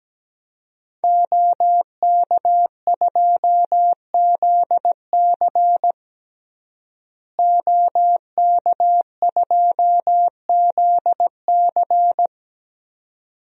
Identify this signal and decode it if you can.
Morse code, heard as a keyed tone, about 17 wpm, 700 Hz. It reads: OK2ZC OK2ZC
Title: DXCC-100 Call Signs - 5 Characters - Repeated 2x - 17wpm